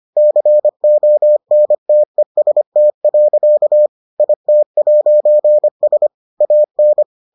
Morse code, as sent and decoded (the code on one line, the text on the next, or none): -.-. --- -. - . ... - .-.-.- / .. - .----. ... / .- -.
CONTEST. IT'S AN